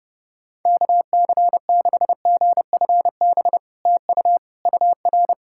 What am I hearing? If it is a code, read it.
Morse code, heard as a keyed tone, about 32 wpm, 700 Hz. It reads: KC6GFB TU UR
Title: SOTA QSO Practice 30wpm